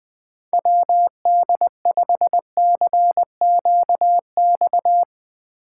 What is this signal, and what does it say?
Morse code, heard as a keyed tone, about 20 wpm, 700 Hz. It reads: WD5CQX
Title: Speed Racing - USA 2x3 Call Signs 17wpm